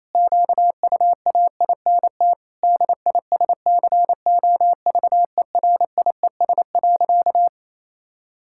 Morse code, heard as a keyed tone, about 28 wpm, 700 Hz. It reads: QUAINT DISCOVERIES.
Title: Pangram Challenge v2 - 28wpm